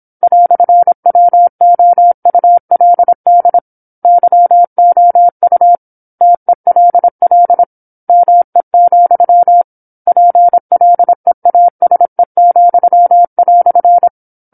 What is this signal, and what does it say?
Morse code, heard as a keyed tone, about 27 wpm, 700 Hz. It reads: "WOULD YOU TELL ME, PLEASE,"
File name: 08-Chapter-08 26 WPM